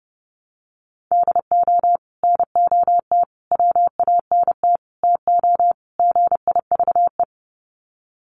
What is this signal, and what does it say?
Morse code, heard as a keyed tone, about 30 wpm, 700 Hz. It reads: DO NOT WANT TO GIVE
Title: Sets of 5 Words Encore 30wpm